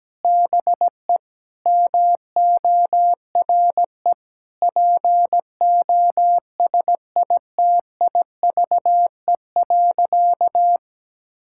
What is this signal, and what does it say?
Morse code, heard as a keyed tone, about 17 wpm, 700 Hz. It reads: BE MORE POSITIVE.
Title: Sentences from Top 1000 Words - Review Part 5 - 17wpm